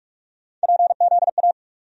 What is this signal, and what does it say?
Morse code, heard as a keyed tone, about 47 wpm, 700 Hz. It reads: PZA
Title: USA Call Sign Postfixes - 3 Letters 45wpm